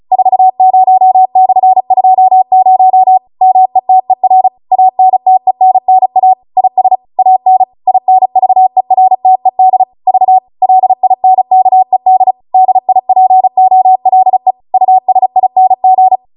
Morse code, heard as a keyed tone, about 36 wpm, 750 Hz. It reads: 40/20 METER ANTENNA IS AN INVERTED V LINKED DIPOLE USING